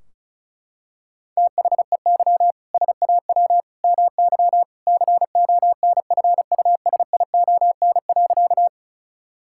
Morse code, heard as a keyed tone, about 34 wpm, 700 Hz. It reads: THEY SAW MY CONFUSION.
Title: Sentences from Top 1600 Words 35wpm